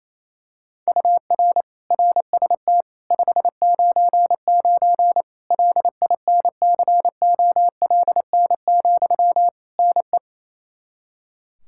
Morse code, heard as a keyed tone, about 28 wpm, 700 Hz. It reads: UR RST 599 LINCOLN, NE
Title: Practice QSO - Exchange 1 - Beginning - 28wpm